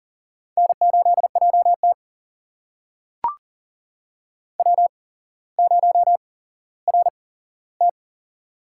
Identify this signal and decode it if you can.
Morse code, heard as a keyed tone, about 39 wpm, spaced as for 17 wpm, 690 Hz. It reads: N8JT W0RT
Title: USA Call Signs - 4 Characters - 8x Character Spacing - 40wpm